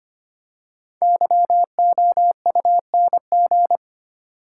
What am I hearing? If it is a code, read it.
Morse code, heard as a keyed tone, about 25 wpm, 700 Hz. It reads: YOUNG
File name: 5 Letter Words - First 3 Letters Given - 25wpm